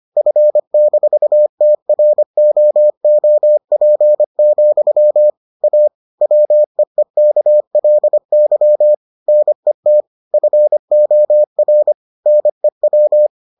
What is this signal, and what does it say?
Morse code, heard as a keyed tone, about 25 wpm, 600 Hz. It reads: F-TROOP, A WEEKLY NET FOR NEW